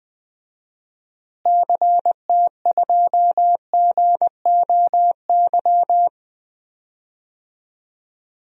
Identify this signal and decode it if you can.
Morse code, heard as a keyed tone, about 20 wpm, 700 Hz. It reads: CT2GOY
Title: DXCC-100 Call Signs - 6 Characters - Repeated 2x - 20wpm